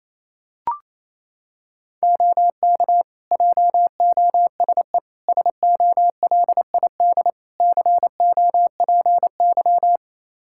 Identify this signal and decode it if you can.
Morse code, heard as a keyed tone, about 28 wpm, 700 Hz. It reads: OK JOSE SOLID COPY
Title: K1USN SST QSO Practice 28wpm